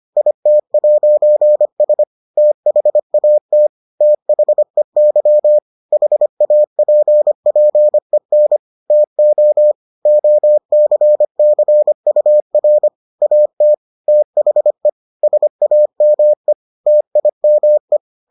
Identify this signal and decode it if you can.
Morse code, heard as a keyed tone, about 25 wpm, 600 Hz. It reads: IT'S THAT THEY HAPPEN TO OCCUR AT THE SAME TIME